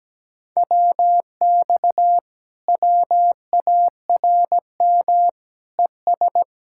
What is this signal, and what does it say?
Morse code, heard as a keyed tone, about 17 wpm, 700 Hz. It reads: WX WARM ES
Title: Speed-Racing Common QSO Phrases - 17wpm